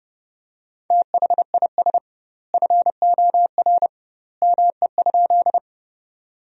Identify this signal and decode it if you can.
Morse code, heard as a keyed tone, about 31 wpm, 700 Hz. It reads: THIS FOR ME?
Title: 2x Word-Spacing - Sentences from Top 200 Words - 30wpm